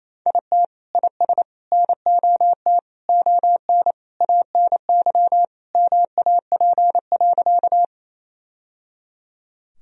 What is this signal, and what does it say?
Morse code, heard as a keyed tone, about 27 wpm, 700 Hz. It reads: IT IS NOT ON ANY MAP.
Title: Sentences from Top 1000 Words - Review Part 2 - 28wpm